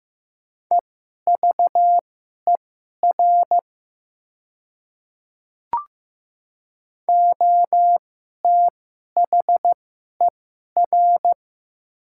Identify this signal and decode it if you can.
Morse code, heard as a keyed tone, about 15 wpm, spaced as for 11 wpm, 700 Hz. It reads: EVER OTHER
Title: 2x Character Spacing - Top 500 Words - 15wpm